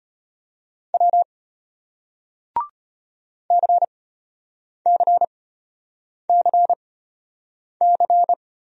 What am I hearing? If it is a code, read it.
Morse code, heard as a keyed tone, about 33 wpm, spaced as for 8 wpm, 690 Hz. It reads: W CCCC